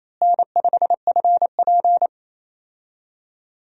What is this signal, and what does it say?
Morse code, heard as a keyed tone, about 28 wpm, 700 Hz. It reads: N5FP
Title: USA Call Signs - 4 Characters - Repeated 2x - 28wpm